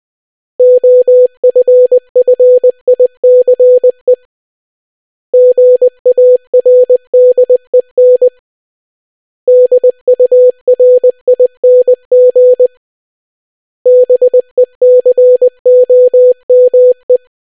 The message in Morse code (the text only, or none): OFFICE GARDEN DURING BECOME